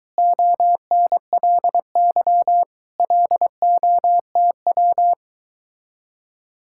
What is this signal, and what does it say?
Morse code, heard as a keyed tone, about 23 wpm, 700 Hz. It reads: ONLY LOTW